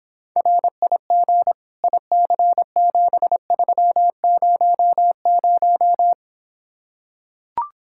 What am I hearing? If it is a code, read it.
Morse code, heard as a keyed tone, about 26 wpm, 700 Hz. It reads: RIG IC7300